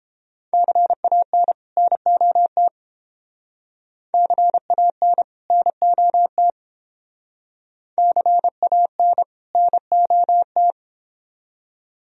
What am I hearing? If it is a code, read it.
Morse code, heard as a keyed tone, about 29 wpm, 700 Hz. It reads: CAN NOT CAN NOT CAN NOT